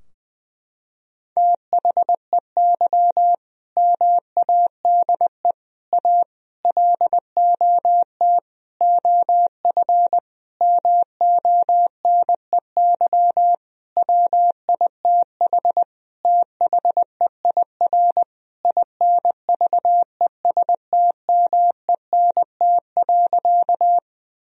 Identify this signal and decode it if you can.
Morse code, heard as a keyed tone, about 20 wpm, 700 Hz. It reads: THEY MADE A LOT OF MONEY WITH THEIR INVESTMENT.
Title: Sentences from Top 1000 Words - Review Part 4 - 20wpm